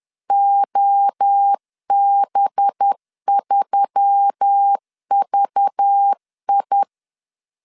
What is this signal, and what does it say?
Morse code, heard as a keyed tone, about 10 wpm, 800 Hz. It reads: OB3VI
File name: LEZIONI / 6 PKX0 / 6 PK 10wpm 2436